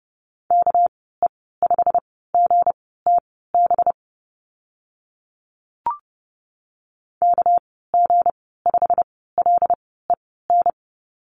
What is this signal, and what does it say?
Morse code, heard as a keyed tone, about 30 wpm, spaced as for 17 wpm, 700 Hz. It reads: KE5GTB KG5LEN
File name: Lesson 37g - Letter x - Call Signs 30wpm